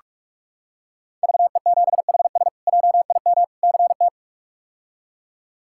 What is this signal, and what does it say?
Morse code, heard as a keyed tone, about 46 wpm, 700 Hz. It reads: VE7HS JIM CT
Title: K1USN SST QSO Practice - 45wpm